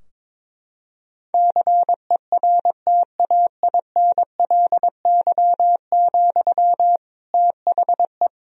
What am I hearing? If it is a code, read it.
Morse code, heard as a keyed tone, about 22 wpm, 700 Hz. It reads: CERTAINLY, THE
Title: Sentences from Top 1000 Words - Review Part 3 - 22wpm